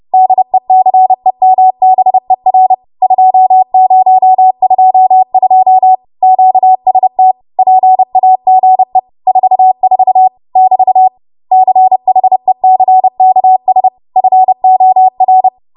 Morse code, heard as a keyed tone, about 30 wpm, 750 Hz. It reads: DECEMBER 2022 QST PAGE 44 <BT> CHECKS FOR
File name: Archive / 30 WPM / 30WPM